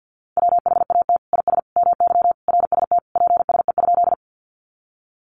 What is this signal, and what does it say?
Morse code, heard as a keyed tone, about 50 wpm, 700 Hz. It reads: WHAT IS MY RST PSE?
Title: Common QSO Phrases 50wpm